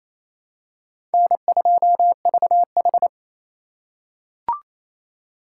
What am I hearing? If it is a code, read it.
Morse code, heard as a keyed tone, about 28 wpm, 700 Hz. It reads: N2VH
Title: USA Call Signs - 2x Character Spacing - 28wpm